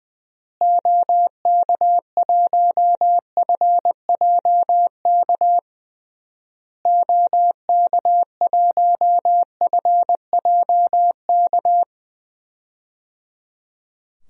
Morse code, heard as a keyed tone, about 20 wpm, 700 Hz. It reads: OK1FJK OK1FJK
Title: DXCC-100 Call Signs - 6 Characters - Repeated 3x - 20wpm